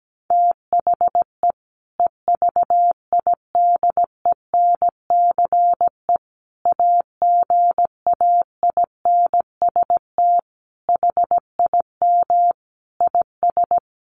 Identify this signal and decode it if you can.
Morse code, heard as a keyed tone, about 17 wpm, 700 Hz. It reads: THE EVIDENCE AGAINST HIM IS
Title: Sentences from Top 1300 Words 17wpm